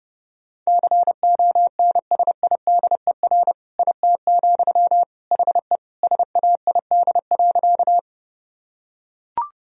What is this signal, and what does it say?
Morse code, heard as a keyed tone, about 32 wpm, 700 Hz. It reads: CONSIDER IT, HE SAID.
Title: Sentences from Top 1000 Words - Review Part 4 - 30wpm